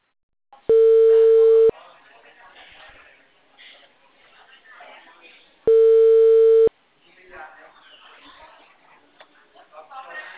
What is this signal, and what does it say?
Ringback tone (China, 450 Hz), with speech.
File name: out- -2030- - - .1071